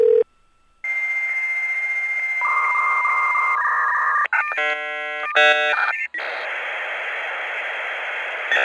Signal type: dial-up modem handshake